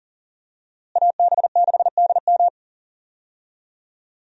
Morse code, heard as a keyed tone, about 40 wpm, 700 Hz. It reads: AB6DM
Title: USA Call Signs - 5 Characters - Repeated 2x - 40wpm